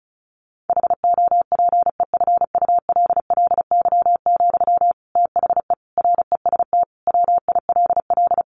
Morse code, heard as a keyed tone, about 34 wpm, 700 Hz. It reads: HOPEFULLY, THE REST WILL
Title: Sentences from Top 1500 Words 35wpm